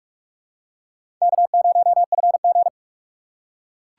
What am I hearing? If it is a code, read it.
Morse code, heard as a keyed tone, about 41 wpm, 700 Hz. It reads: K0FG
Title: USA Call Signs - 4 Characters - 2x Character Spacing - 45wpm